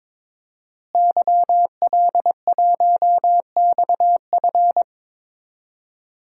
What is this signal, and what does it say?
Morse code, heard as a keyed tone, about 22 wpm, 700 Hz. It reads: YL1XF
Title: DXCC-100 Call Signs - 5 Characters - Repeated 3x - 22wpm